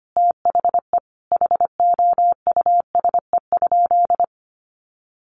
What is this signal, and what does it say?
Morse code, heard as a keyed tone, about 25 wpm, 700 Hz. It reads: THE HOUSE?
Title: Sentences from Top 200 Words 25wpm